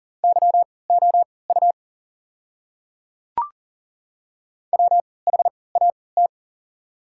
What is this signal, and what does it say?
Morse code, heard as a keyed tone, about 38 wpm, spaced as for 22 wpm, 690 Hz. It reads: YOU WHAT
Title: 3x Character Spacing - Top 100 Words - 40wpm